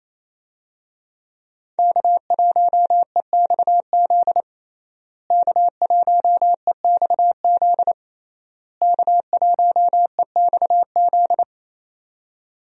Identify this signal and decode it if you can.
Morse code, heard as a keyed tone, about 28 wpm, 700 Hz. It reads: K1EXZ K1EXZ K1EXZ
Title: USA Call Signs - 5 Characters - Repeated 3x - 28wpm